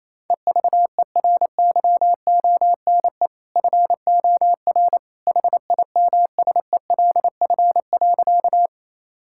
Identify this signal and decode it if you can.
Morse code, heard as a keyed tone, about 28 wpm, 700 Hz. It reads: EVERYONE FOR HIMSELF.
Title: Sentences from Top 1600 Words 28wpm